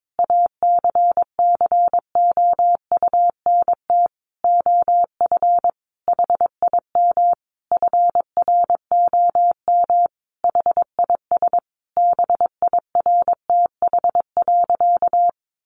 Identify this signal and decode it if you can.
Morse code, heard as a keyed tone, about 22 wpm, 700 Hz. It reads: ACCOUNT OF HIM FROM HIS BIRTH.
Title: Sentences from Top 900 Words 22wpm